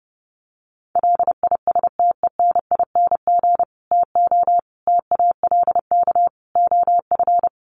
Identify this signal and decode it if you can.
Morse code, heard as a keyed tone, about 30 wpm, 700 Hz. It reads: LISTENING TO TALK OF
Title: Sets of 5 Words 30wpm